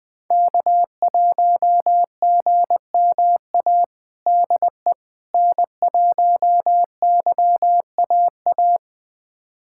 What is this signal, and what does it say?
Morse code, heard as a keyed tone, about 20 wpm, 700 Hz. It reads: K1GMA DE N1YAA
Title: USA Callsign DE Callsign - 5 Characters - 20wpm